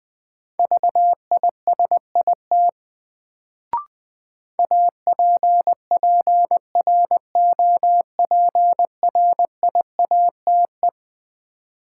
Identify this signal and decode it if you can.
Morse code, heard as a keyed tone, about 20 wpm, 700 Hz. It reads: VISIT APPROPRIATE
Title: Sentences from Top 900 Words - New Word Drill - 20wpm